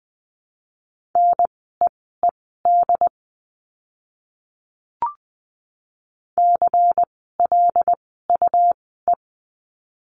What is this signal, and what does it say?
Morse code, heard as a keyed tone, about 20 wpm, spaced as for 15 wpm, 700 Hz. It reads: NEED CLUE
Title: 101 Lesson 17f - Letter u - Words 20wpm